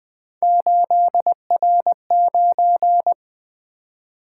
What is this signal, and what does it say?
Morse code, heard as a keyed tone, about 20 wpm, 700 Hz. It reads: 8R9
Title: Three Letter-Number Combinations - 20wpm